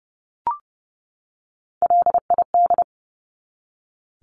Morse code, heard as a keyed tone, about 30 wpm, 700 Hz. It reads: LID
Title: Abbreviations 30wpm